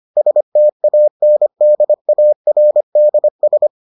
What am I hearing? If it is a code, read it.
Morse code, heard as a keyed tone, about 25 wpm, 600 Hz. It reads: STANDARDS